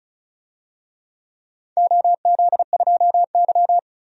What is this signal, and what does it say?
Morse code, heard as a keyed tone, about 36 wpm, 700 Hz. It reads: OZ2Y
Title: DXCC-100 Call Signs - 4 Characters - 35wpm